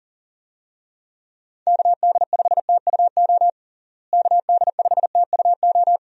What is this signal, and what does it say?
Morse code, heard as a keyed tone, about 37 wpm, 700 Hz. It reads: KD5TUO KD5TUO
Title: USA Call Signs - 6 Characters - Repeated 2x - 40wpm